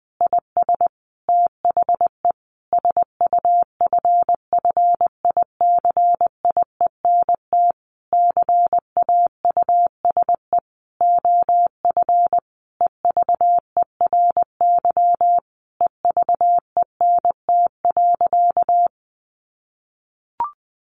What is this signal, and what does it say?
Morse code, heard as a keyed tone, about 20 wpm, 700 Hz. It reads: IS THE SUFFICIENT CAUSE OF EVERY EVENT.
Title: Sentences from Top 1400 Words 20wpm